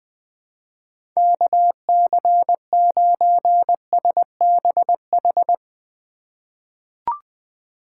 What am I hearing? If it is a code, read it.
Morse code, heard as a keyed tone, about 20 wpm, 700 Hz. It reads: KC9SBH
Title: USA Call Signs - 6 Characters - 8x Character Spacing - 20wpm